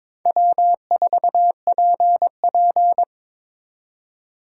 Morse code, heard as a keyed tone, about 26 wpm, 700 Hz. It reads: W4PP
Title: Speed Racing - USA 1x2 Call Signs 22wpm